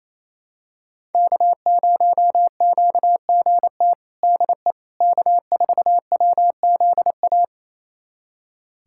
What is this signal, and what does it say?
Morse code, heard as a keyed tone, about 27 wpm, 700 Hz. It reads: K0QGT DE K4WZA
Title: USA Callsign DE Callsign - 5 Characters - 28wpm